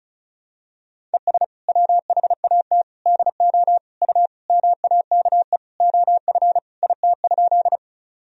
Morse code, heard as a keyed tone, about 36 wpm, 700 Hz. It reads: ES WHAT DO U MAKE OF IT?
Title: Sentences from Top 100 Words and Abbreviations 35wpm